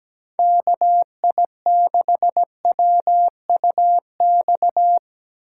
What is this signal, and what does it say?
Morse code, heard as a keyed tone, about 17 wpm, 700 Hz. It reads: KI6WUX
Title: USA Call Signs - 6 Characters - 2x Character Spacing - 17wpm